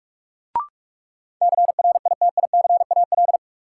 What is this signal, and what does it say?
Morse code, heard as a keyed tone, about 46 wpm, 700 Hz. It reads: CRITICAL
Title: Sentences from Top 1100 Words - New Word Drill - 45wpm